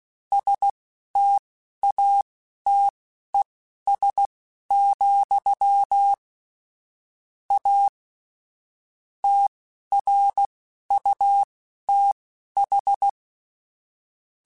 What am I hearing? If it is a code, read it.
Morse code, heard as a keyed tone, about 16 wpm, spaced as for 11 wpm, 800 Hz. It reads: STATES, A TRUTH
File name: Technical Subjects / Technical Subjects-mp3s / AND Gate @10